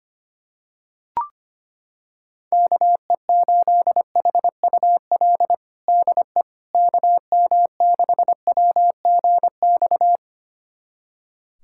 Morse code, heard as a keyed tone, about 25 wpm, 700 Hz. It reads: KE8HUL DE KM6WGX
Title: USA Callsign DE Callsign 25wpm